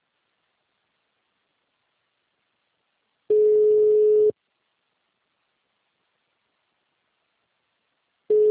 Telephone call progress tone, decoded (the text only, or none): ringback tone